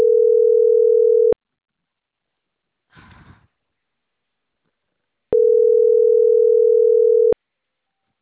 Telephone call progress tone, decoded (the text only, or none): ringback tone